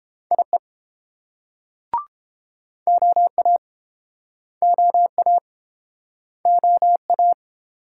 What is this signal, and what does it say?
Morse code, heard as a keyed tone, about 32 wpm, 700 Hz. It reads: IE OA OA OA